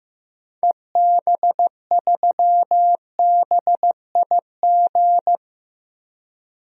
Morse code, heard as a keyed tone, about 16 wpm, 700 Hz. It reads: EB3BIG
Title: DXCC-100 Call Signs - 6 Characters - 15wpm